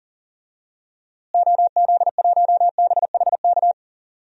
Morse code, heard as a keyed tone, about 40 wpm, 700 Hz. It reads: OZ1BHK
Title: DXCC-100 Call Signs - 6 Characters - Repeated 3x - 40wpm